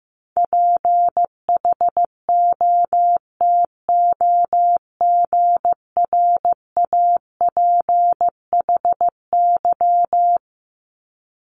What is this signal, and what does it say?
Morse code, heard as a keyed tone, about 15 wpm, 700 Hz. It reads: PHOTOGRAPHY
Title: Multi-letter Phonemes 15wpm